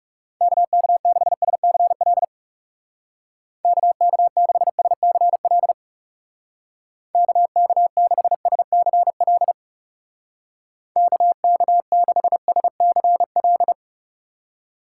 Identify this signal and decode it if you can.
Morse code, heard as a keyed tone, about 36 wpm, 700 Hz. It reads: KK6SCL KK6SCL KK6SCL KK6SCL
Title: Speed Racing - USA 2x3 Call Signs 30wpm